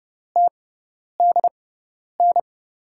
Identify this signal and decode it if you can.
Morse code, heard as a keyed tone, about 31 wpm, spaced as for 10 wpm, 700 Hz. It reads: TDN
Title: USA Call Signs - 5 Characters - 6x Character Spacing - 30wpm